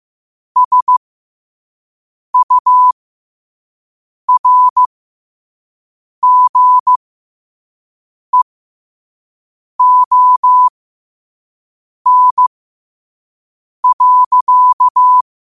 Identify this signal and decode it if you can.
Morse code, heard as a keyed tone, about 15 wpm, spaced as for 5 wpm, 1000 Hz. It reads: SURGEON.
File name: qotdfiles / 5wpm